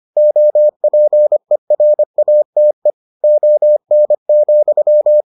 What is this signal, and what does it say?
Morse code, heard as a keyed tone, about 25 wpm, 600 Hz. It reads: OPERATE ON,